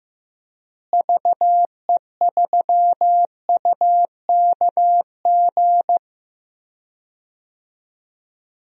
Morse code, heard as a keyed tone, about 15 wpm, 700 Hz. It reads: VE3UKG
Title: DXCC-100 Call Signs - 6 Characters - Repeated 3x - 15wpm